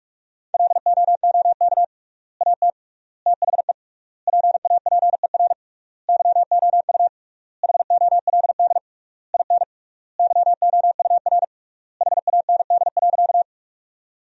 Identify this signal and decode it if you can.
Morse code, heard as a keyed tone, about 46 wpm, 700 Hz. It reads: LOOK AT THE PAPER YOU HOLD IN YOUR HAND.
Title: 3x Word-Spacing - Sentences from Top 400 Words - 45wpm